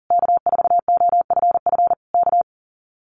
Morse code, heard as a keyed tone, about 40 wpm, 700 Hz. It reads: K4OFF K
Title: Common QSO Phrases Encore 40wpm